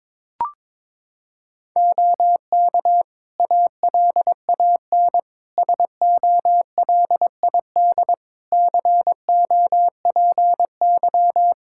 Morse code, heard as a keyed tone, about 22 wpm, 700 Hz. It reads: OK ALAN SOLID COPY
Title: Practice QSO - Exchange 3 - More Info - 22wpm